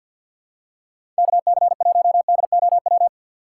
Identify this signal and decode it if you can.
Morse code, heard as a keyed tone, about 47 wpm, 700 Hz. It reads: KC1DOW
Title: USA Call Signs - 6 Characters - 6x Character Spacing - 50wpm